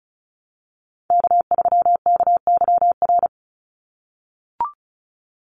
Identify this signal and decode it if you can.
Morse code, heard as a keyed tone, about 34 wpm, 700 Hz. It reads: K3KYR
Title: NA SOTA Chaser Callsigns 35wpm